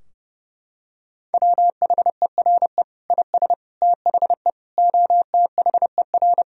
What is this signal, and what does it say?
Morse code, heard as a keyed tone, about 30 wpm, 700 Hz. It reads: WHERE IS THE OTHER
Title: Sentences from Top 1000 Words - Review Part 4 - 30wpm